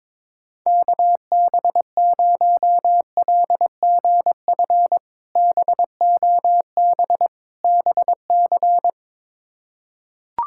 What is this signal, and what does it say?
Morse code, heard as a keyed tone, about 22 wpm, 700 Hz. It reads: KB0LGF BOB BC
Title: K1USN SST QSO Practice - 22wpm